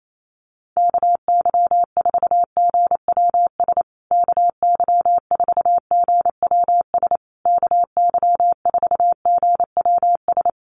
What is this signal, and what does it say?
Morse code, heard as a keyed tone, about 28 wpm, 700 Hz. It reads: KY4GWS KY4GWS KY4GWS
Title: USA Call Signs Repeated 3x 28wpm